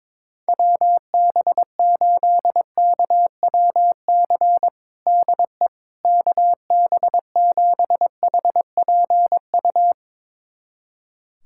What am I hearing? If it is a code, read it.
Morse code, heard as a keyed tone, about 22 wpm, 700 Hz. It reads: WB8KWC DE KB7HPU
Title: USA Callsign DE Callsign - 6 Characters - 22wpm